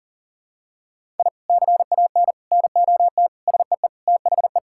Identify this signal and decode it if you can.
Morse code, heard as a keyed tone, about 40 wpm, 700 Hz. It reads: I CAN NOT SEE THE
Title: Sentences from Top 1000 Words - Review Part 6 - 40wpm